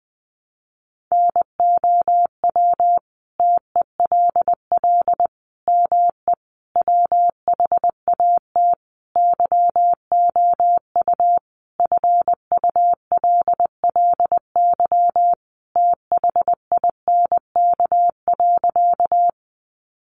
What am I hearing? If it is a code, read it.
Morse code, heard as a keyed tone, about 20 wpm, 700 Hz. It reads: NOW TELL ME WHAT YOU FULLY THINK.
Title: Sentences from Top 1300 Words 20wpm